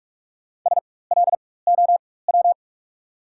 Morse code, heard as a keyed tone, about 42 wpm, spaced as for 21 wpm, 690 Hz. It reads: SLOW
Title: 4x Character Spacing - Top 500 Words - 45wpm